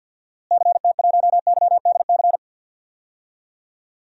Morse code, heard as a keyed tone, about 49 wpm, 700 Hz. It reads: CT1YDC
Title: DXCC-100 Call Signs - 6 Characters - Repeated 3x - 50wpm